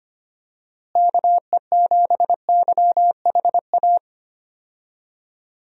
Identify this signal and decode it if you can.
Morse code, heard as a keyed tone, about 25 wpm, 700 Hz. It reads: KE7YHA
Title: USA Call Signs - 6 Characters - Repeated 3x - 25wpm